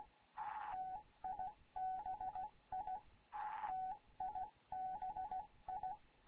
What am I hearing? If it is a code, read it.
Morse code, heard as a keyed tone, about 16 wpm, 750 Hz. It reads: TIBI TIBI
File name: Morze-HA2MN / hangp / 2 11 GYAKORLAT